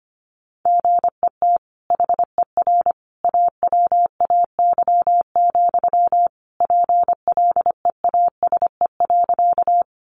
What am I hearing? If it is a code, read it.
Morse code, heard as a keyed tone, about 25 wpm, 700 Hz. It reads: GET HER AWAY, PLEASE.
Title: Sentences from Top 1500 Words 25wpm